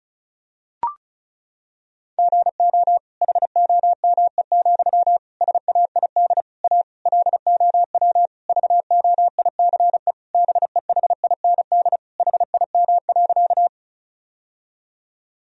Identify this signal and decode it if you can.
Morse code, heard as a keyed tone, about 34 wpm, 700 Hz. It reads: GO HOME, SAID A LOW VOICE BEHIND HIM.
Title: Sentences from Top 1000 Words - Review Part 1 - 35wpm